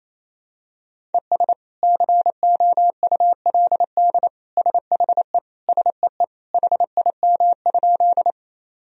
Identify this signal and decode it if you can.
Morse code, heard as a keyed tone, about 28 wpm, 700 Hz. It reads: ES COULD SHE SEE HIM?
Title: Sentences from Top 100 Words and Abbreviations 28wpm